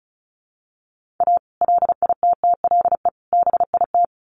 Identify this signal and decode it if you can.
Morse code, heard as a keyed tone, about 34 wpm, 700 Hz. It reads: A LITTLE BIT
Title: Sets of 3 Words 35wpm